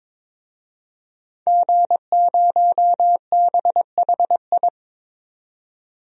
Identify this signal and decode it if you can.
Morse code, heard as a keyed tone, about 21 wpm, 700 Hz. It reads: G0BHI
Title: DXCC-100 Call Signs - 5 Characters - 22wpm